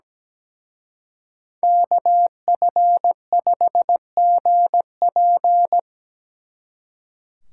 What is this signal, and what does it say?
Morse code, heard as a keyed tone, about 20 wpm, 700 Hz. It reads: KF5GP
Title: Speed Racing - USA Call Signs 17wpm